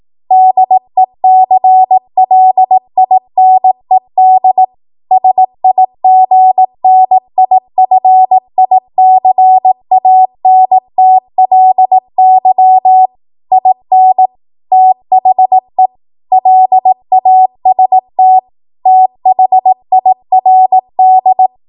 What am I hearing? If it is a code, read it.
Morse code, heard as a keyed tone, about 18 wpm, 750 Hz. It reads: DECLINED SIGNIFICANTLY IN THE LAST THIRD
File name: Morse / 18WPM